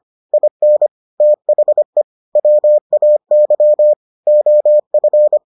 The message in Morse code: IN THE WAY OF

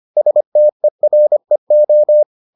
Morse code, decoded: STEREO

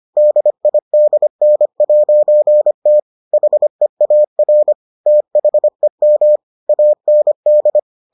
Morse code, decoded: DIDN'T HEAR THEM AND

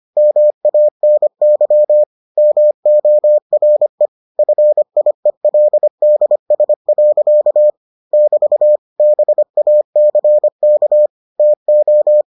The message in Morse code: MANY MORE FIELDS. <BT> BACK TO

